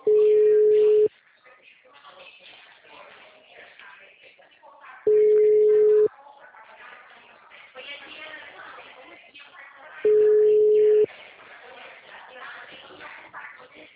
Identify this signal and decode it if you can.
Ringback tone (Europe, 425 Hz), with speech.
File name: out- -2012- - - .176